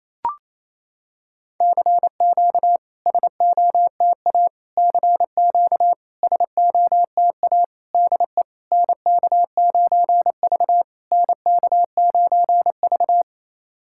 Morse code, heard as a keyed tone, about 28 wpm, 700 Hz. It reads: CQ SOTA CQ SOTA DE NK9V NK9V
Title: SOTA QSO Practice 28wpm